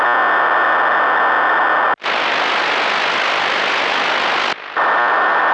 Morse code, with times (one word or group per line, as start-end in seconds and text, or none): none